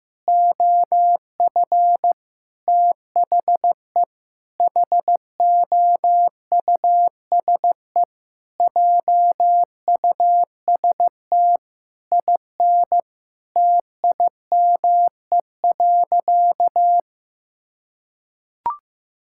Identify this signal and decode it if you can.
Morse code, heard as a keyed tone, about 14 wpm, 700 Hz. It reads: OF THE HOUSE JUST IN TIME.
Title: Sentences from Top 1600 Words 15wpm